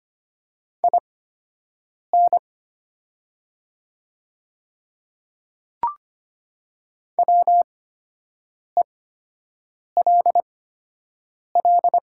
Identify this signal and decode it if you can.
Morse code, heard as a keyed tone, about 25 wpm, spaced as for 7 wpm, 700 Hz. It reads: IN WELL
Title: 8x Character Spacing - Top 200 Words - 25wpm